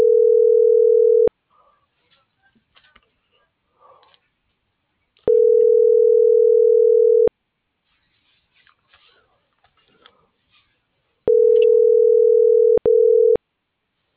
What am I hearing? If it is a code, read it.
Ringback tone (North America), 440 + 480 Hz.
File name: CENTRAL SAGOT / out danae.bsagot - .14241